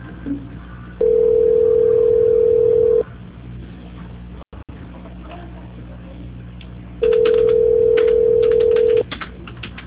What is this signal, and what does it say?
Ringback tone (North America), 440 + 480 Hz.
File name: out- -2045- - - .22416